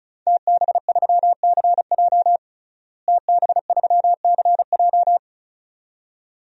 Morse code, heard as a keyed tone, about 36 wpm, 700 Hz. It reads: TB3CJ TB3CJ
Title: DXCC-100 Call Signs - 5 Characters - Repeated 2x - 35wpm